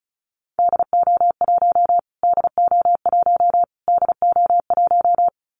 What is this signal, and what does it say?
Morse code, heard as a keyed tone, about 34 wpm, 700 Hz. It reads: DO1 DO1 DO1
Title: Callsign Prefixes for Top 25 DXCC - Repeated 3x - 35wpm